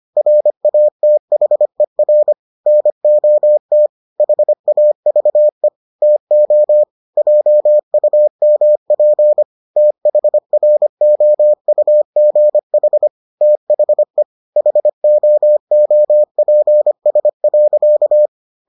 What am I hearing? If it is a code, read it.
Morse code, heard as a keyed tone, about 25 wpm, 600 Hz. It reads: RATHER NOT HAVE TO JUMP THROUGH THE HOOPS.